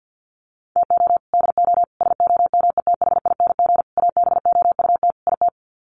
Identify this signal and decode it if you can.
Morse code, heard as a keyed tone, about 49 wpm, 700 Hz. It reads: TO DO SOMETHING ABOUT IT
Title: Sets of 5 Words 50wpm